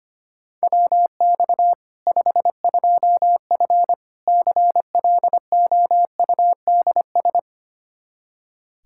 Morse code, heard as a keyed tone, about 25 wpm, 700 Hz. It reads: WX 52F CLOUDS
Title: K1USN SST QSO Practice 25wpm